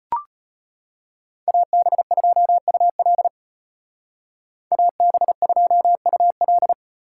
Morse code, heard as a keyed tone, about 33 wpm, 700 Hz. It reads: AB2UL AB2UL